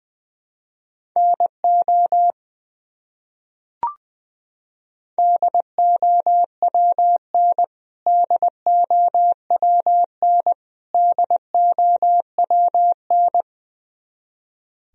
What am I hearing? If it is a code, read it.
Morse code, heard as a keyed tone, about 20 wpm, 700 Hz. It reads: NO DOWN DOWN DOWN
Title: Top 100 Words Repeated 20wpm